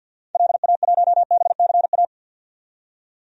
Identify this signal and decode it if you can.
Morse code, heard as a keyed tone, about 48 wpm, 700 Hz. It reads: LA1BCA
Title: DXCC-100 Call Signs - 6 Characters - 50wpm